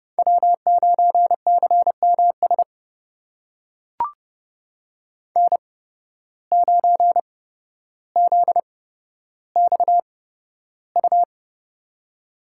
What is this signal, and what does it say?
Morse code, heard as a keyed tone, about 30 wpm, spaced as for 13 wpm, 700 Hz. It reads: W9CMS N9ZXU
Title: USA Call Signs - 5 Characters - 8x Character Spacing - 30wpm